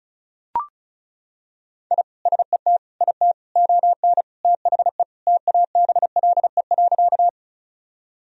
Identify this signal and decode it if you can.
Morse code, heard as a keyed tone, about 36 wpm, 700 Hz. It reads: I SET IT ON THE TABLE.
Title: Sentences from Top 1000 Words - Review Part 2 - 35wpm